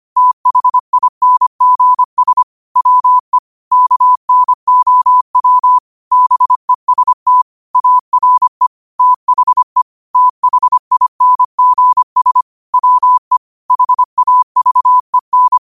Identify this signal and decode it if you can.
Morse code, heard as a keyed tone, about 25 wpm, 1000 Hz. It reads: THINGS WE KNOW BEST ARE THE THINGS WE HAVEN
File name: qotdfiles / 25wpm